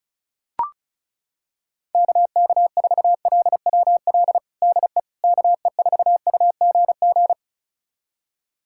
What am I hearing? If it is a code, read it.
Morse code, heard as a keyed tone, about 34 wpm, 700 Hz. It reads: KK4LWL DE KE4UGG
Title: USA Callsign DE Callsign - 6 Characters - 35wpm